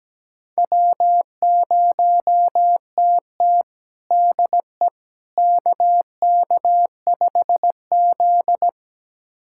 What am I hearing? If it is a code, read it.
Morse code, heard as a keyed tone, about 17 wpm, 700 Hz. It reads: W0TT DE KK5Z
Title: USA Callsign DE Callsign - 4 Characters - 17wpm